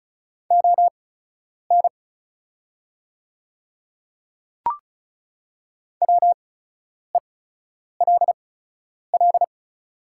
Morse code, heard as a keyed tone, about 36 wpm, spaced as for 10 wpm, 700 Hz. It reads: ON WELL
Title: 8x Character Spacing - Top 300 Words - 35wpm